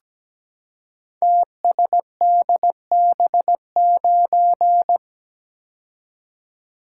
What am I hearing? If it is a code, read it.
Morse code, heard as a keyed tone, about 17 wpm, 700 Hz. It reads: TSDB9
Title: Code Groups - 5 Letter:Numbers - 17wpm